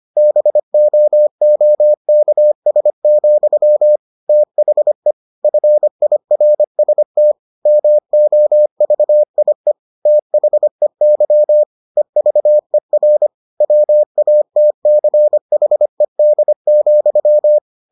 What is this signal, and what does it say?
Morse code, heard as a keyed tone, about 25 wpm, 600 Hz. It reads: BOOKS, THE FIRST MOVIE THEY EVER WATCHED,